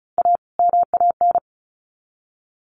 Morse code, heard as a keyed tone, about 34 wpm, 700 Hz. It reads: A MAN
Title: Sets of 2 Words 35wpm